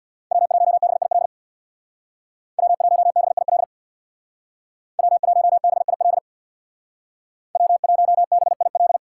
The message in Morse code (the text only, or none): W1BIL W1BIL W1BIL W1BIL